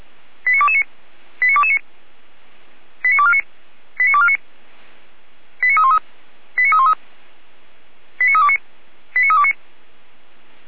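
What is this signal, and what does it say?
Five-tone selective call (ZVEI), heard as quick runs of five tones, tones of about 70 ms. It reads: 89229 89229 89249 89249 89313 89313 89329 89329
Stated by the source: ZVEI five-tone selective call 89276 received on SDR.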